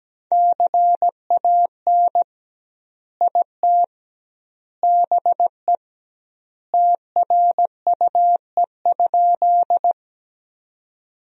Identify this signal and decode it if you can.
Morse code, heard as a keyed tone, about 17 wpm, 700 Hz. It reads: CAN IT BE TRUE?
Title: 2x Word-Spacing - Sentences from Top 400 Words - 17wpm